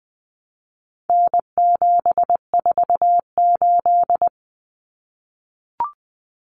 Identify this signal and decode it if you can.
Morse code, heard as a keyed tone, about 20 wpm, 700 Hz. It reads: N748
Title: Common Cut Numbers - 4-Digit 20wpm